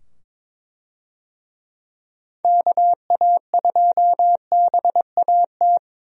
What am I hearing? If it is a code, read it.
Morse code, heard as a keyed tone, about 22 wpm, 700 Hz. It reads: KA2BAT
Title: USA Call Signs - 6 Characters - 22wpm